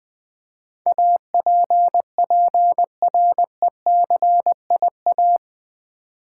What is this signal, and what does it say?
Morse code, heard as a keyed tone, about 20 wpm, 700 Hz. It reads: APPRECIA
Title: Build A Word - 4 Letters At A Time - 12-Letter Words - 20wpm